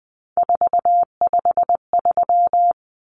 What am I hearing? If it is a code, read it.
Morse code, heard as a keyed tone, about 20 wpm, 700 Hz. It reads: 453
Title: Common Cut Numbers - 3-Digit 20wpm